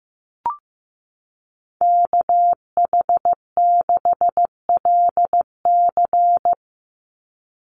Morse code, heard as a keyed tone, about 15 wpm, 700 Hz. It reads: KH6LC
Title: Call Signs Repeated 3x - 15wpm